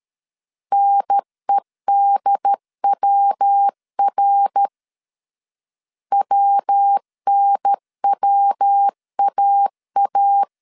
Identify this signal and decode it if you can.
Morse code, heard as a keyed tone, about 13 wpm, 800 Hz. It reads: NEDWR WNWAA
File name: CW / DET / 1 3 1500C S B 12wpm 2828